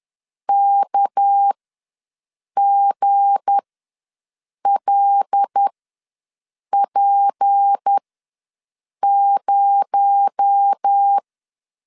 Morse code, heard as a keyed tone, about 10 wpm, spaced as for 6 wpm, 800 Hz. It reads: KGLP0
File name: LEZIONI / 7 GF9Z / 7 G ws 10wpm 3843